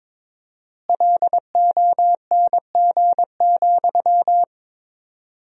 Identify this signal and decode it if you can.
Morse code, heard as a keyed tone, about 22 wpm, 700 Hz. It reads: LONG,
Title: 4x Word-Spacing - Sentences from Top 500 Words - 22wpm